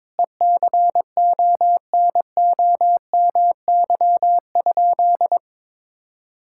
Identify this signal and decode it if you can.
Morse code, heard as a keyed tone, about 22 wpm, 700 Hz. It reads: ECONOMY?
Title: Sentences from Top 1000 Words - Review Part 6 - 22wpm